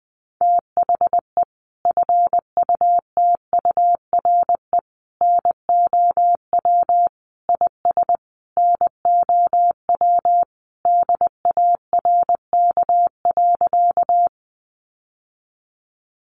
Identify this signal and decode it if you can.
Morse code, heard as a keyed tone, about 19 wpm, 700 Hz. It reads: THE FUTURE NOW IS NOW DARK.
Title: Sentences from Top 700 Words 20wpm